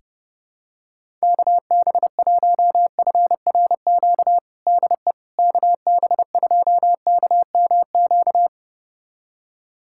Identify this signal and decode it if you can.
Morse code, heard as a keyed tone, about 30 wpm, 700 Hz. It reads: KB1FRQ DE KB2KMQ
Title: USA Callsign DE Callsign - 6 Characters - 30wpm